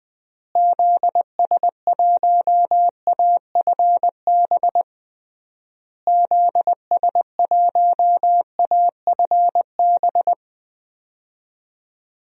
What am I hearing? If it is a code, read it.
Morse code, heard as a keyed tone, about 20 wpm, 700 Hz. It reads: ZS1AFB ZS1AFB
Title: DXCC-100 Call Signs - 6 Characters - Repeated 3x - 20wpm